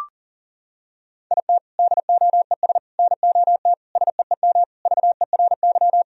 Morse code, heard as a keyed tone, about 40 wpm, 700 Hz. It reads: IT DOES NOT SEEM VERY
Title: Sentences from Top 1000 Words - Review Part 1 - 40wpm